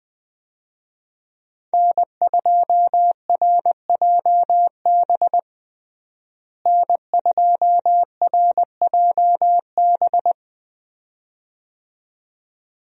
Morse code, heard as a keyed tone, about 20 wpm, 700 Hz. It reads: N2RJB N2RJB
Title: USA Call Signs - 5 Characters - Repeated 2x - 20wpm